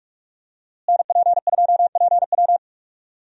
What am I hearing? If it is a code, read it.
Morse code, heard as a keyed tone, about 46 wpm, 700 Hz. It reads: NP2PW
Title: USA Call Signs - 5 Characters - Repeated 3x - 45wpm